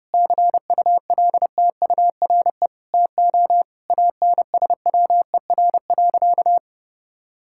Morse code, heard as a keyed tone, about 32 wpm, 700 Hz. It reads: CULTURE TO ANSWER.
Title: Sentences from Top 1000 Words - Review Part 6 - 30wpm